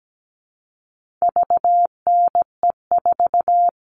Morse code, heard as a keyed tone, about 17 wpm, 700 Hz. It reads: VNE4
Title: All Cut Numbers - 4-Digit 17wpm